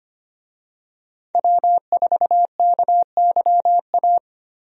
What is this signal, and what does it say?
Morse code, heard as a keyed tone, about 25 wpm, 700 Hz. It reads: W4KYA
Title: USA Call Signs - 5 Characters - 25wpm